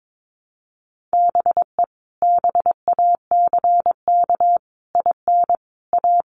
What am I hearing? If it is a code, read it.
Morse code, heard as a keyed tone, about 22 wpm, 700 Hz. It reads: BE BACK IN A
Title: Sets of 5 Words 22wpm